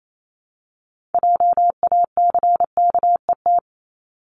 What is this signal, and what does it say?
Morse code, heard as a keyed tone, about 28 wpm, 700 Hz. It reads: JACKET
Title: Multi-letter Phonemes Repeated 3x 28wpm